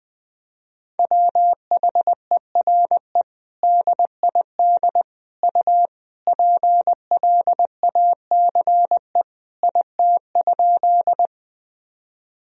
Morse code, heard as a keyed tone, about 20 wpm, 700 Hz. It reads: WHERE DID U PLACE IT?
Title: Sentences from Top 200 Words and Abbreviations 20wpm